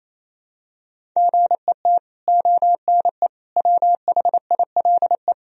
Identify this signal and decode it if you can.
Morse code, heard as a keyed tone, about 28 wpm, 700 Hz. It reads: GET ONE WHILE
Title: Sentences from Top 1000 Words - Review Part 1 - 28wpm